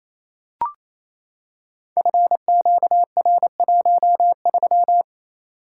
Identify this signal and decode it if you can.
Morse code, heard as a keyed tone, about 27 wpm, 700 Hz. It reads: FQR13
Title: Code Groups - 5 Letter:Numbers - 28wpm